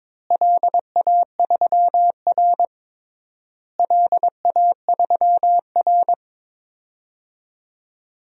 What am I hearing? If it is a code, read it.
Morse code, heard as a keyed tone, about 22 wpm, 700 Hz. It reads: LA3R LA3R
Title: DXCC-100 Call Signs - 4 Characters - Repeated 3x - 22wpm